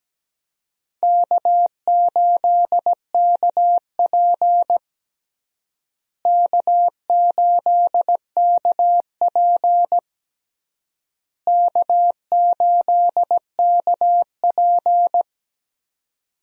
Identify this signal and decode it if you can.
Morse code, heard as a keyed tone, about 17 wpm, 700 Hz. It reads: K8KP K8KP K8KP
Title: USA Call Signs - 4 Characters - Repeated 3x - 17wpm